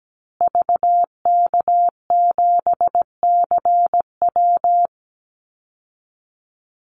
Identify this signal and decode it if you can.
Morse code, heard as a keyed tone, about 17 wpm, 700 Hz. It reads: VK7CW
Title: Call Signs Repeated 2x - 17wpm